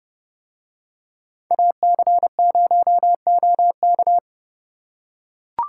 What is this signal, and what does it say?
Morse code, heard as a keyed tone, about 29 wpm, 700 Hz. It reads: AC0OK